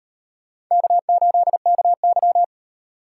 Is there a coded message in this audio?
Yes: Morse code from 0.7 to 2.4 s, heard as a keyed tone, about 38 wpm, 700 Hz. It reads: K8KY